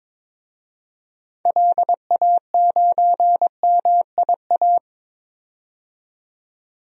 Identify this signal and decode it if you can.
Morse code, heard as a keyed tone, about 21 wpm, 700 Hz. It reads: LA9MIA
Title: DXCC-100 Call Signs - 6 Characters - 22wpm